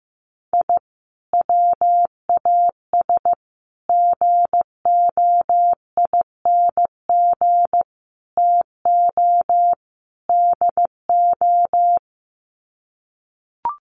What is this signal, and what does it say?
Morse code, heard as a keyed tone, about 15 wpm, 700 Hz. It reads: I WAS GOING TO DO
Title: Sets of 5 Words Encore2 15wpm